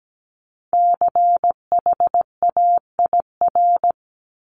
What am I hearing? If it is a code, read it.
Morse code, heard as a keyed tone, about 17 wpm, 700 Hz. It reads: CHAIR
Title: Multi-letter Phonemes 17wpm